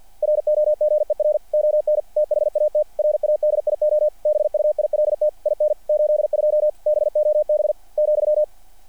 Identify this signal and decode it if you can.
Morse code, heard as a keyed tone, about 48 wpm, 600 Hz. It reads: ROGER ON THAT RADIO BUILT IN 82 BOB ,
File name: archived / CRSnet